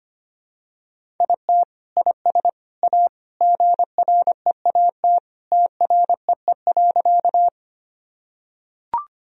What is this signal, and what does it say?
Morse code, heard as a keyed tone, about 25 wpm, 700 Hz. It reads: IT IS A GREAT TREE.
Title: Sentences from Top 1000 Words - Review Part 1 - 25wpm